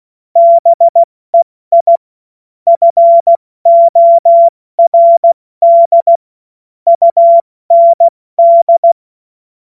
Morse code, heard as a keyed tone, about 16 wpm, 680 Hz. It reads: BEI FORD UND
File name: Witz-14-Oktober-2025-70-F80